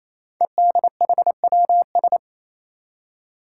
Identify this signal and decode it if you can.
Morse code, heard as a keyed tone, about 28 wpm, 700 Hz. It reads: EDHWS
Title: Five Challenging Letter-Number Combinations 28wpm